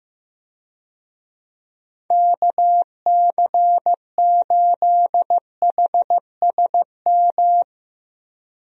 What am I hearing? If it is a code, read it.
Morse code, heard as a keyed tone, about 15 wpm, 700 Hz. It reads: KC8HSM
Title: USA Call Signs - 6 Characters - Repeated 2x - 15wpm